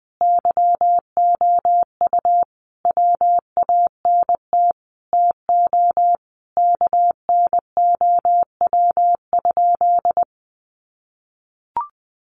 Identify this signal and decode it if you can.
Morse code, heard as a keyed tone, about 20 wpm, 700 Hz. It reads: YOU WANT TO KNOW?
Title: Sentences from Top 200 Words 20wpm